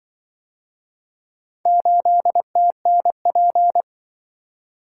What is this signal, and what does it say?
Morse code, heard as a keyed tone, about 24 wpm, 700 Hz. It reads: 8TNP
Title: Speed-Racing - Four Letter-Number Combinations - 20wpm